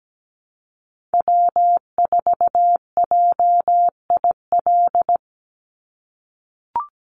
Morse code, heard as a keyed tone, about 17 wpm, 700 Hz. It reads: W4JIL
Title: USA Call Signs 17wpm